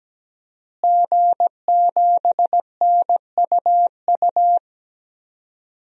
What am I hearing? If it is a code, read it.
Morse code, heard as a keyed tone, about 17 wpm, 700 Hz. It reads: G7NUU
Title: DXCC-100 Call Signs - 5 Characters - Repeated 2x - 17wpm